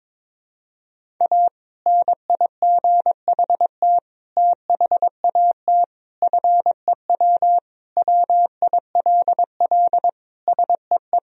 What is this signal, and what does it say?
Morse code, heard as a keyed tone, about 22 wpm, 700 Hz. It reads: A NIGHT THAT FEW WILL SEE
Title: Sentences from Top 1000 Words - Review Part 1 - 22wpm